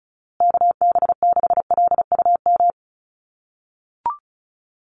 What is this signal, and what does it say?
Morse code, heard as a keyed tone, about 34 wpm, 700 Hz. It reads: KB6LUM
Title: USA Call Signs Repeated 2x 35wpm